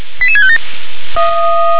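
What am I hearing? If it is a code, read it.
Five-tone selective call (ZVEI), heard as quick runs of five tones, tones of about 70 ms. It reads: 88657
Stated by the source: ZVEI five-tone selective call received by SDR.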